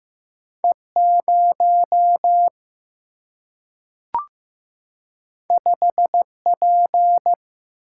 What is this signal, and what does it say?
Morse code, heard as a keyed tone, about 15 wpm, 700 Hz. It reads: E 0 5 P
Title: Single-Letter Single-Number 15wpm